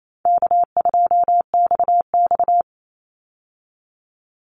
Morse code, heard as a keyed tone, about 28 wpm, 700 Hz. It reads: K2XX
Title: Easy Call Signs Repeated 3x - 28wpm